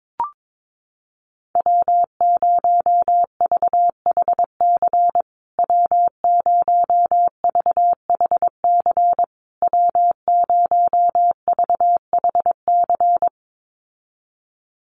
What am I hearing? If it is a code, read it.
Morse code, heard as a keyed tone, about 22 wpm, 700 Hz. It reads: W0VHC W0VHC W0VHC
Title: USA Call Signs Repeated 3x 22wpm